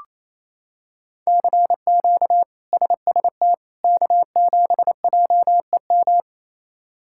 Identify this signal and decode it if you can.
Morse code, heard as a keyed tone, about 27 wpm, 700 Hz. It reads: CQ SST K7JEM
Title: K1USN SST QSO Practice - 28wpm